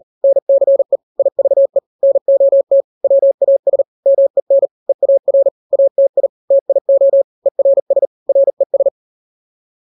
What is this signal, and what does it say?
Morse code, heard as a keyed tone, about 40 wpm, 550 Hz. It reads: NCE IVE NOT WAS MEN EAR ATI TIO ERS RES
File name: n-grams / 3-grams 40 35 WPM